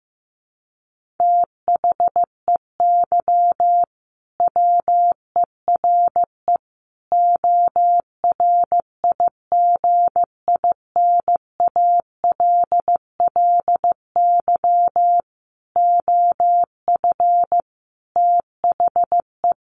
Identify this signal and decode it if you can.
Morse code, heard as a keyed tone, about 15 wpm, 700 Hz. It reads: THEY WERE ORIGINALLY OF THE
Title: Sentences from Top 1500 Words 15wpm